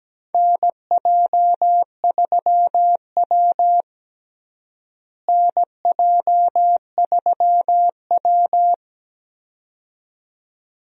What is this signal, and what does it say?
Morse code, heard as a keyed tone, about 17 wpm, 700 Hz. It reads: NJ3W NJ3W
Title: USA Call Signs - 4 Characters - Repeated 2x - 17wpm